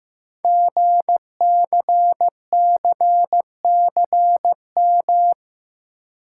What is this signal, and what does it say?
Morse code, heard as a keyed tone, about 15 wpm, 700 Hz. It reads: GCCCM
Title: Code Groups - 5 Letters - 15wpm